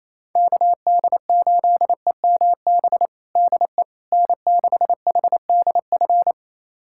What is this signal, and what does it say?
Morse code, heard as a keyed tone, about 28 wpm, 700 Hz. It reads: KD8EMB DE N6HDF
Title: USA Callsign DE Callsign 28wpm